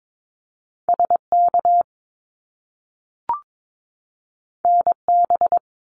Morse code, heard as a keyed tone, about 22 wpm, 700 Hz. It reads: SK NB
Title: Canadian Provinces and Territories 22wpm